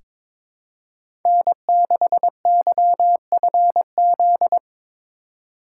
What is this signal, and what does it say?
Morse code, heard as a keyed tone, about 22 wpm, 700 Hz. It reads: N6YFZ
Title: USA Call Signs - 5 Characters - 4x Character Spacing - 22wpm